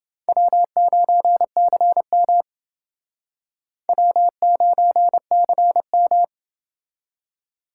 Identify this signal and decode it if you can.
Morse code, heard as a keyed tone, about 29 wpm, 700 Hz. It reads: W9CM W9CM